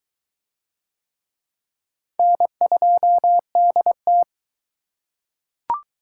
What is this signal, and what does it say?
Morse code, heard as a keyed tone, about 22 wpm, 700 Hz. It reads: N2DT